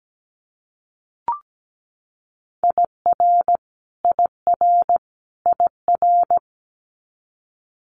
Morse code, heard as a keyed tone, about 17 wpm, 700 Hz. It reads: IR IR IR
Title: Multi-letter Phonemes Repeated 3x 17wpm